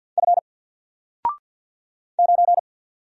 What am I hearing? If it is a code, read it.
Morse code, heard as a keyed tone, about 48 wpm, 700 Hz. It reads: F 9
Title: Single Character (Letter:Number:Punctuation:Prosign) 50wpm